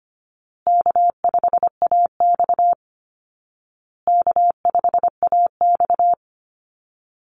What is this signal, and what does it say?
Morse code, heard as a keyed tone, about 25 wpm, 700 Hz. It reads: K5AX K5AX
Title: Easy Call Signs Repeated 3x - 25wpm